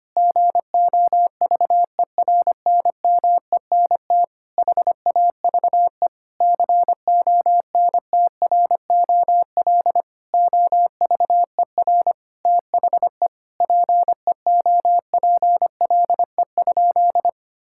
Morse code, heard as a keyed tone, about 25 wpm, 700 Hz. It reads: GOVERNMENT HAVE CONTROL OVER THE PEOPLE?
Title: Sentences from Top 1000 Words - Review Part 5 - 25wpm